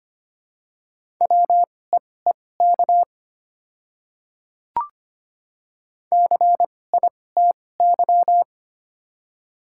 Morse code, heard as a keyed tone, about 25 wpm, spaced as for 18 wpm, 700 Hz. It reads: WEEK CITY
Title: 2x Character Spacing - Top 500 Words - 25wpm